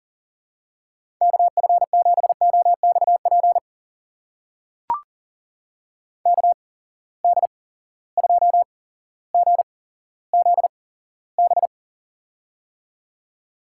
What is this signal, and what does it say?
Morse code, heard as a keyed tone, about 40 wpm, spaced as for 17 wpm, 700 Hz. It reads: KF7OXP KD2GZB
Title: USA Call Signs - 8x Character Spacing - 40wpm